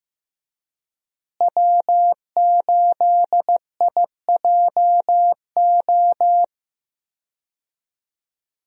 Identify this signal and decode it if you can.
Morse code, heard as a keyed tone, about 15 wpm, 700 Hz. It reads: W8IJO
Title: USA Call Signs - 5 Characters - Repeated 3x - 15wpm